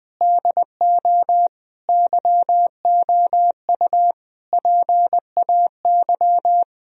Morse code, heard as a keyed tone, about 20 wpm, 700 Hz. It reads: DO YOU PAY
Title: Sentences from Top 1000 Words - Review Part 3 - 20wpm